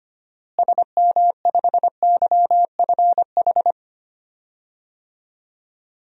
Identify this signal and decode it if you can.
Morse code, heard as a keyed tone, about 25 wpm, 700 Hz. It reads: SM5YFH
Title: DXCC-100 Call Signs - 6 Characters - Repeated 2x - 25wpm